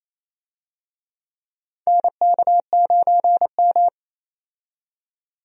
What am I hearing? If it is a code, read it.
Morse code, heard as a keyed tone, about 28 wpm, 690 Hz. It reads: NK9M
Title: USA Call Signs - 4 Characters - 28wpm